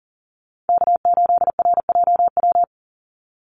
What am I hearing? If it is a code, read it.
Morse code, heard as a keyed tone, about 38 wpm, 700 Hz. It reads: K8RJW
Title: Call Signs Repeated 3x - 40wpm